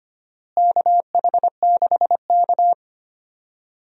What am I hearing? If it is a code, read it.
Morse code, heard as a keyed tone, about 25 wpm, 700 Hz. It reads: KH6K
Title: DXCC-100 Call Signs - 4 Characters - Repeated 3x - 25wpm